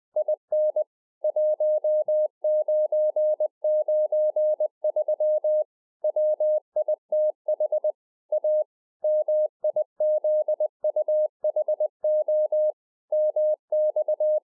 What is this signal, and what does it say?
Morse code, heard as a keyed tone, about 20 wpm, 610 Hz. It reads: IN 1993 WITH A MIZUHO MX
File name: 2011-04-mp3 / A SMALL TRANCEIVER 3 20WPM